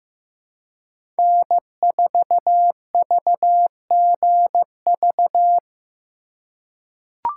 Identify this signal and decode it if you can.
Morse code, heard as a keyed tone, about 15 wpm, 700 Hz. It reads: N4VGV
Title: USA Call Signs - 5 Characters - 3x Character Spacing - 15wpm